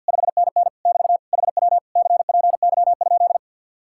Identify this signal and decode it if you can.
Morse code, heard as a keyed tone, about 63 wpm, 700 Hz. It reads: 5NN <BT> HW CPY?